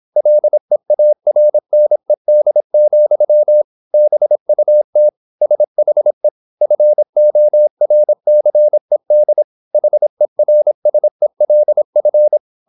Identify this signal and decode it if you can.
Morse code, heard as a keyed tone, about 26 wpm, 600 Hz. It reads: LEARNED, BUT SHE FORCED HERSELF
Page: Never 26wpm | CW med Gnister